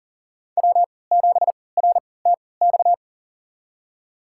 Morse code, heard as a keyed tone, about 39 wpm, spaced as for 22 wpm, 700 Hz. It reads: W7RTX
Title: USA Call Signs - 5 Characters - 3x Character Spacing - 40wpm